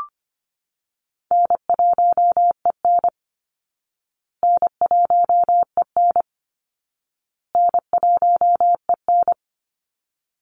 Morse code, heard as a keyed tone, about 25 wpm, 700 Hz. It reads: N1EN N1EN N1EN
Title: Call Signs Repeated 2x - 25wpm